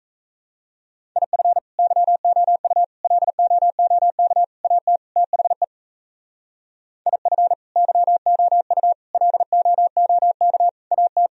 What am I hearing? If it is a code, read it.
Morse code, heard as a keyed tone, about 34 wpm, 700 Hz. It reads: IF YOU LOOK AT THE IF YOU LOOK AT
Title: Speed-Racing - Sets of 5 Words 28wpm